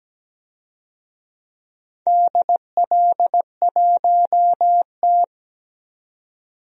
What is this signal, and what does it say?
Morse code, heard as a keyed tone, about 17 wpm, 700 Hz. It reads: DL1T
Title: DXCC-100 Call Signs - 4 Characters - 17wpm